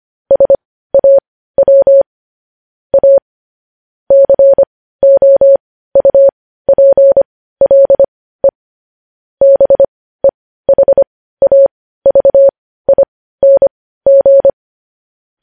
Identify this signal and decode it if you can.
Morse code, heard as a keyed tone, about 25 wpm, spaced as for 15 wpm, 550 Hz. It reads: SAW A COUPLE BEHAVING
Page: News headlines at 15